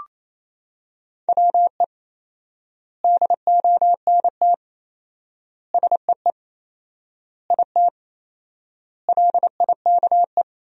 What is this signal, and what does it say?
Morse code, heard as a keyed tone, about 28 wpm, 700 Hz. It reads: WE DONT SEE IT LIKE
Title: 4x Word-Spacing - Sentences from Top 300 Words - 28wpm